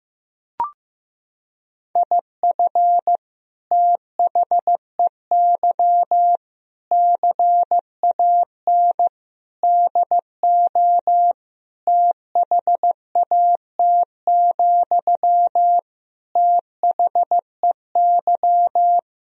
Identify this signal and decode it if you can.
Morse code, heard as a keyed tone, about 16 wpm, 700 Hz. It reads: IF THEY CAN DO THAT, THEY
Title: Sentences from Top 1000 Words - Review Part 4 - 15wpm